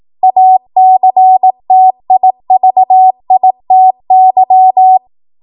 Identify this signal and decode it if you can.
Morse code, heard as a keyed tone, about 18 wpm, 750 Hz. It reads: ACTIVITY
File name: Morse / 18WPM